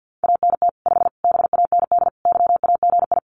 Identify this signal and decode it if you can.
Morse code, heard as a keyed tone, about 50 wpm, 700 Hz. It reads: ANT 5 BAND YAGI
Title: Common QSO Phrases 50wpm